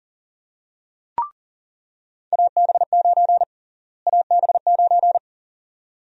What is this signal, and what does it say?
Morse code, heard as a keyed tone, about 40 wpm, 700 Hz. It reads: AB9 AB9
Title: USA Call Sign Prefixes - Repeated 2x - 40wpm